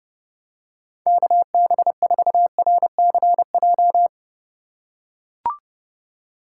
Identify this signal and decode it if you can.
Morse code, heard as a keyed tone, about 30 wpm, 700 Hz. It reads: KB4RCJ
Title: USA Call Signs - 6 Characters - 2x Character Spacing - 30wpm